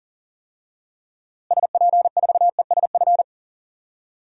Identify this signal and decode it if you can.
Morse code, heard as a keyed tone, about 40 wpm, 700 Hz. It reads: SP4ESF
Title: DXCC-100 Call Signs - 6 Characters - Repeated 3x - 40wpm